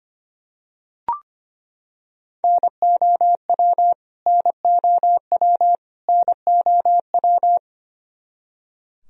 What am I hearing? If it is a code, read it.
Morse code, heard as a keyed tone, about 25 wpm, 700 Hz. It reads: NOW NOW NOW
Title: Top 100 Words Repeated 25wpm